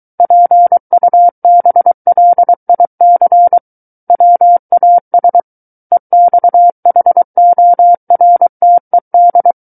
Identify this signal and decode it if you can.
Morse code, heard as a keyed tone, about 23 wpm, 700 Hz. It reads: PUBLIC WAS EXHORTED
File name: Book One / 14-Chapter 14 - 23 WPM